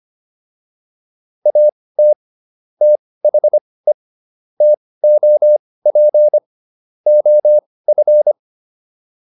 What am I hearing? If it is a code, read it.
Morse code, heard as a keyed tone, about 25 wpm, spaced as for 18 wpm, 600 Hz. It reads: AT THE TOP OF